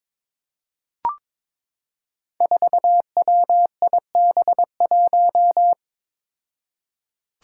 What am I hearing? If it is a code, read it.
Morse code, heard as a keyed tone, about 22 wpm, 700 Hz. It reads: 4WIB1
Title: Five Challenging Letter-Number Combinations 22wpm